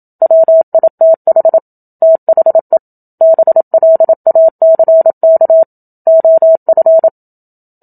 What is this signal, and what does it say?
Morse code, heard as a keyed tone, about 27 wpm, 650 Hz. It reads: WITH THE BLACK OF
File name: Book Two / 01-Chapter 1 - 27 WPM